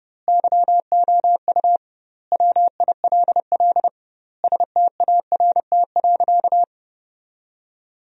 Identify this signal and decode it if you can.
Morse code, heard as a keyed tone, about 32 wpm, 700 Hz. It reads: YOU WILL START.
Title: 2x Word-Spacing - Sentences from Top 300 Words - 30wpm